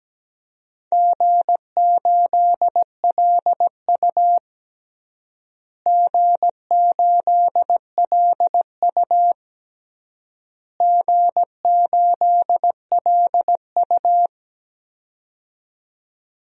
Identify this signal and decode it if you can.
Morse code, heard as a keyed tone, about 17 wpm, 700 Hz. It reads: G8LU G8LU G8LU
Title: DXCC-100 Call Signs - 4 Characters - Repeated 3x - 17wpm